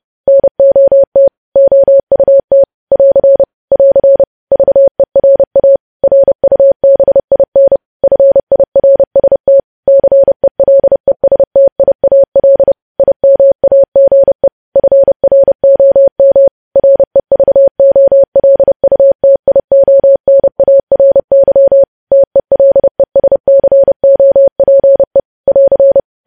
Morse code, heard as a keyed tone, about 30 wpm, 550 Hz. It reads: NOT OUT <AR> <AR> VERA RUBIN FIRST CELESTIAL IMAGE FROM REVOLUTIONARY TELESCOPE <AR>
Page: news headlines in morse code at 30 wpm